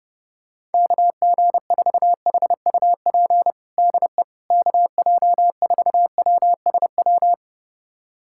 Morse code, heard as a keyed tone, about 32 wpm, 700 Hz. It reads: KG4HUP DE KJ4WSW
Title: USA Callsign DE Callsign - 6 Characters - 30wpm